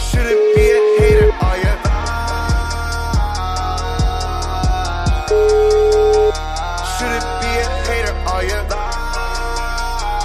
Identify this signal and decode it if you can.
Ringback tone (Europe, 425 Hz), with noise.